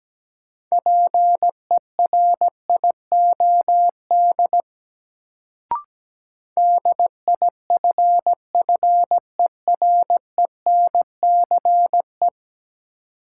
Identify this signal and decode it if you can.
Morse code, heard as a keyed tone, about 17 wpm, 700 Hz. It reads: PERIOD DIFFERENCE
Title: Sentences from Top 800 Words - New Word Drill - 17wpm